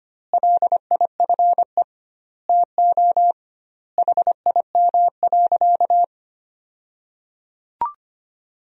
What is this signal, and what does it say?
Morse code, heard as a keyed tone, about 25 wpm, 700 Hz. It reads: LIFE TO HIM.
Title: 2x Word-Spacing - Sentences from Top 300 Words - 25wpm